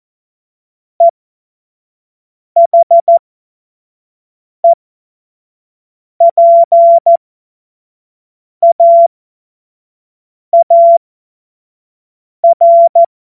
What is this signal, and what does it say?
Morse code, heard as a keyed tone, about 14 wpm, spaced as for 5 wpm, 680 Hz. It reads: EHEPAAR
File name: Morsekurs / Witze / Witz 28. März 2023 25 F70